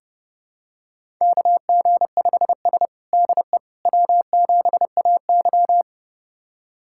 Morse code, heard as a keyed tone, about 30 wpm, 700 Hz. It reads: KG5S DE W7AY
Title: USA Callsign DE Callsign - 4 Characters - 30wpm